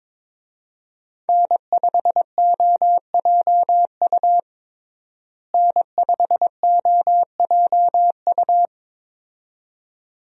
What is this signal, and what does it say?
Morse code, heard as a keyed tone, about 22 wpm, 700 Hz. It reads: N5OJU N5OJU
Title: USA Call Signs - 5 Characters - Repeated 2x - 22wpm